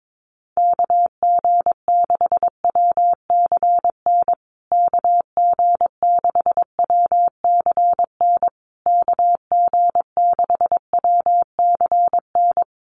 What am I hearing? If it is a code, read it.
Morse code, heard as a keyed tone, about 22 wpm, 700 Hz. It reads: KG6WCN KG6WCN KG6WCN
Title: USA Call Signs Repeated 3x 22wpm